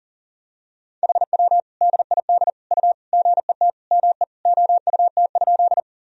Morse code, heard as a keyed tone, about 40 wpm, 700 Hz. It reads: HW DID U GET ME OUT?
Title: Sentences from Top 200 Words and Abbreviations 40wpm